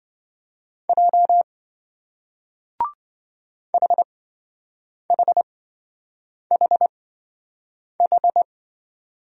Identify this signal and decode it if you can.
Morse code, heard as a keyed tone, about 26 wpm, spaced as for 7 wpm, 700 Hz. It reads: J HHHH